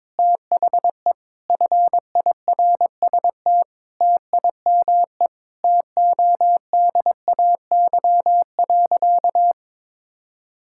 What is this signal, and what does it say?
Morse code, heard as a keyed tone, about 22 wpm, 700 Hz. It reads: THE FIRST TIME TODAY.
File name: practice / 22wpm / Sentences with 4-Word Phrases - 22wpm